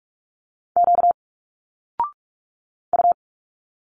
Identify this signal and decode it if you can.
Morse code, heard as a keyed tone, about 46 wpm, 700 Hz. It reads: Q U
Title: Single Letters 45wpm